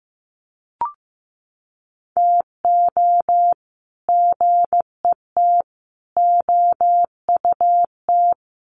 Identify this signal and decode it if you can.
Morse code, heard as a keyed tone, about 15 wpm, 700 Hz. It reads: TO GET OUT
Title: Sets of 4 Words 15wpm